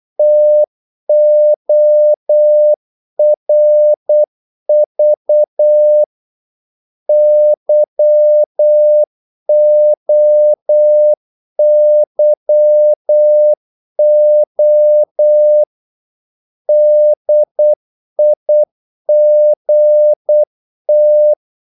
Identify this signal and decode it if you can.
Morse code, heard as a keyed tone, about 8 wpm, 600 Hz. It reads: TORV YOYO DIGT
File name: Korte ord DK 8wpm